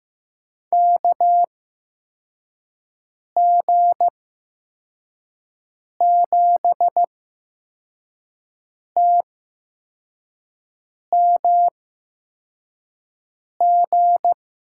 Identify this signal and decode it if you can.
Morse code, heard as a keyed tone, about 15 wpm, spaced as for 4 wpm, 700 Hz. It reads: KG7TMG
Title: USA Call Signs - 6 Characters - 8x Character Spacing - 15wpm